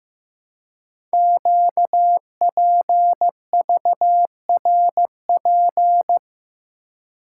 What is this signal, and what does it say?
Morse code, heard as a keyed tone, about 15 wpm, 700 Hz. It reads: QPVRP
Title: Code Groups - 5 Letters - 15wpm